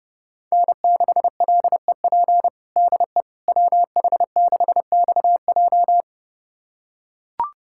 Morse code, heard as a keyed tone, about 30 wpm, 700 Hz. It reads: N6LEP DE WH6XJ
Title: USA Callsign DE Callsign - 5 Characters - 30wpm